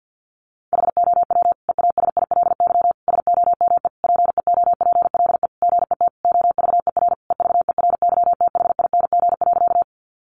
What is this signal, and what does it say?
Morse code, heard as a keyed tone, about 50 wpm, 700 Hz. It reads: HOW EASILY SOME PEOPLE GET OVER EVERYTHING.
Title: Sentences from Top 900 Words 50wpm